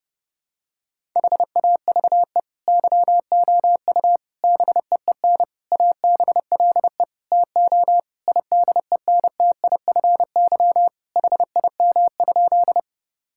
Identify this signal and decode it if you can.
Morse code, heard as a keyed tone, about 30 wpm, 700 Hz. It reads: HAVE YOU BEEN ABLE TO IDENTIFY HIM?
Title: Sentences from Top 1000 Words - Review Part 6 - 30wpm